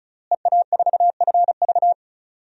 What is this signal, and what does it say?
Morse code, heard as a keyed tone, about 36 wpm, 700 Hz. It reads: EA4FV
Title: DXCC-100 Call Signs - 5 Characters - Repeated 2x - 35wpm